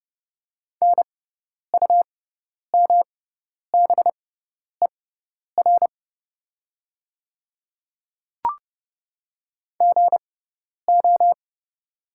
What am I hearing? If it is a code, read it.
Morse code, heard as a keyed tone, about 32 wpm, spaced as for 10 wpm, 700 Hz. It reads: NUMBER GO
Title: 6x Character Spacing - Top 100 Words - 30wpm